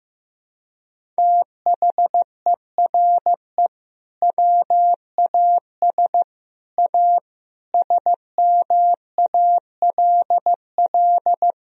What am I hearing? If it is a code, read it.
Morse code, heard as a keyed tone, about 15 wpm, 700 Hz. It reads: THERE WAS A SMALL
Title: Sentences from Top 1000 Words - Review Part 1 - 15wpm